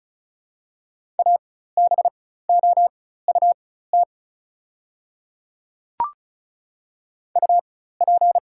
Morse code, heard as a keyed tone, about 36 wpm, spaced as for 16 wpm, 700 Hz. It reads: ABOUT UP
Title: 4x Character Spacing - Top 100 Words - 35wpm